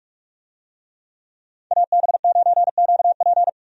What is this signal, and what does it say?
Morse code, heard as a keyed tone, about 46 wpm, 700 Hz. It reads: AB9QP
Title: USA Call Signs - 5 Characters - 45wpm